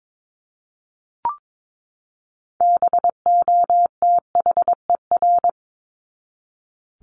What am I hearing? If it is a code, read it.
Morse code, heard as a keyed tone, about 22 wpm, 700 Hz. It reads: BOTHER
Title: Top 1501-1750 Words 22wpm